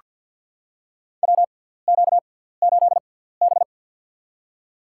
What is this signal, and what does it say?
Morse code, heard as a keyed tone, about 46 wpm, spaced as for 17 wpm, 700 Hz. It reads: WQ8B
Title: USA Call Signs - 4 Characters - 6x Character Spacing - 50wpm